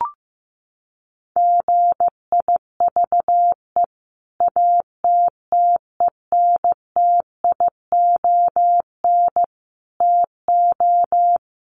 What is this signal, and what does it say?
Morse code, heard as a keyed tone, about 15 wpm, 700 Hz. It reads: GIVE ATTENTION TO
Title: Sentences from Top 1000 Words 15wpm